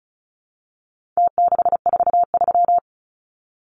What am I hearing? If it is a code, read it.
Morse code, heard as a keyed tone, about 34 wpm, 700 Hz. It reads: T643
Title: Numbers - Contest Style - 4-Digit 35wpm